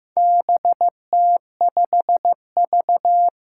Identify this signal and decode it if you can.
Morse code, heard as a keyed tone, about 15 wpm, 700 Hz. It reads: BT5V
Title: Four Challenging Letter-Number Combinations 15wpm